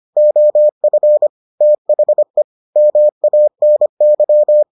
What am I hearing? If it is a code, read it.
Morse code, heard as a keyed tone, about 25 wpm, 600 Hz. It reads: OF THE MANY